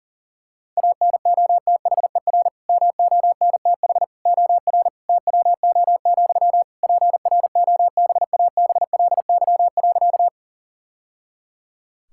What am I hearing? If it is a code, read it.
Morse code, heard as a keyed tone, about 40 wpm, 700 Hz. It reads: ANOTHER MONTH OR TWO, PROBABLY.
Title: Sentences from Top 1000 Words - Review Part 5 - 40wpm